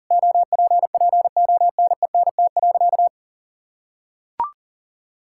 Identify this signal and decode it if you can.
Morse code, heard as a keyed tone, about 40 wpm, 700 Hz. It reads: OPPONENT.
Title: Pangram Challenge v2 - 40wpm